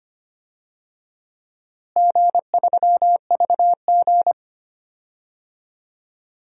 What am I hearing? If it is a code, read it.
Morse code, heard as a keyed tone, about 25 wpm, 700 Hz. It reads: G3VG
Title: DXCC-100 Call Signs - 4 Characters - Repeated 2x - 25wpm